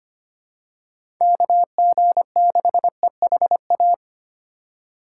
Morse code, heard as a keyed tone, about 25 wpm, 700 Hz. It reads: KG6EHA
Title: USA Call Signs - 6 Characters - 4x Character Spacing - 25wpm